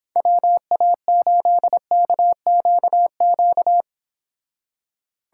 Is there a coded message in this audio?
Yes: Morse code from 0.2 to 3.8 s, heard as a keyed tone, about 26 wpm, 700 Hz. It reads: WA8KQQ